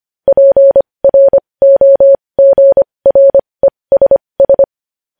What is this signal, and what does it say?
Morse code, heard as a keyed tone, about 25 wpm, spaced as for 20 wpm, 550 Hz. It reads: PROGRESS
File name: podcast / News headlines-1x-25x20